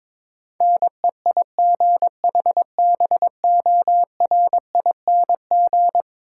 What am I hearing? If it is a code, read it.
Morse code, heard as a keyed tone, about 22 wpm, 700 Hz. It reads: NEIGHBORING
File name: practice / 22wpm / Eleven Letter Words - 22wpm